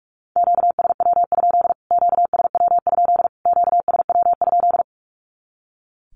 Morse code, heard as a keyed tone, about 44 wpm, 700 Hz. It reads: QSW? QSW? QSW?
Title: Q-Codes Repeated 3x 45wpm